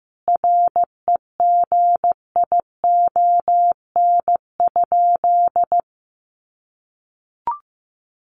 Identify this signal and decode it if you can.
Morse code, heard as a keyed tone, about 15 wpm, 700 Hz. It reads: REGION?
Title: Sentences from Top 1500 Words 15wpm